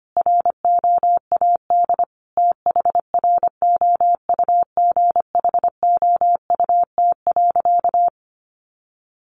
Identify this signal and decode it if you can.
Morse code, heard as a keyed tone, about 25 wpm, 700 Hz. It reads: ROAD THROUGHOUT.
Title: Sentences from Top 800 Words 25wpm